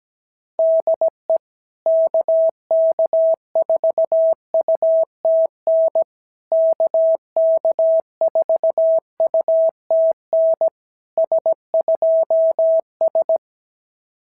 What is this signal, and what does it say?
Morse code, heard as a keyed tone, about 17 wpm, 650 Hz. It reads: DE KK4UTN KK4UTN S2S
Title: SOTA QSO Practice 17wpm